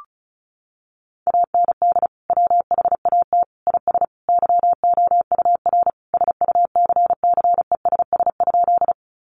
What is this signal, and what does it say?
Morse code, heard as a keyed tone, about 34 wpm, 700 Hz. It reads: AND WHAT IS YOUR SUCCESS?
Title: Sentences from Top 900 Words 35wpm